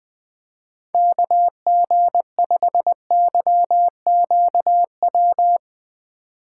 Morse code, heard as a keyed tone, about 20 wpm, 700 Hz. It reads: KG5YQW
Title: Speed Racing - USA 2x3 Call Signs 17wpm